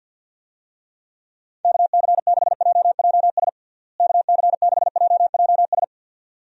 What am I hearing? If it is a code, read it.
Morse code, heard as a keyed tone, about 52 wpm, 700 Hz. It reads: KC6JJS KC6JJS
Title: USA Call Signs - 6 Characters - Repeated 2x - 50wpm